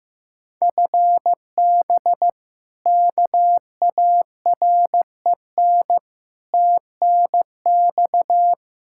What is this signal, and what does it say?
Morse code, heard as a keyed tone, about 15 wpm, 700 Hz. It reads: FB KAREN TNX
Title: K1USN SST QSO Practice 15wpm